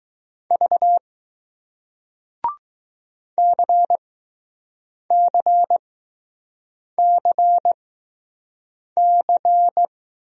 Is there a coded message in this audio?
Yes: Morse code from 0.5 to 9.8 s, heard as a keyed tone, about 20 wpm, spaced as for 6 wpm, 700 Hz. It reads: V CCCC